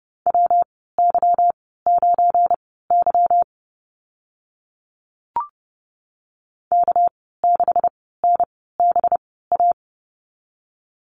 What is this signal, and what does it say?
Morse code, heard as a keyed tone, about 30 wpm, spaced as for 17 wpm, 700 Hz. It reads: WY9Y K6NBA
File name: Lesson 37g - Letter x - Call Signs 30wpm